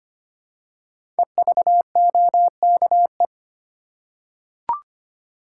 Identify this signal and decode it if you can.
Morse code, heard as a keyed tone, about 25 wpm, 700 Hz. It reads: EVOKE
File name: 5 Letter Words - First 3 Letters Given - 25wpm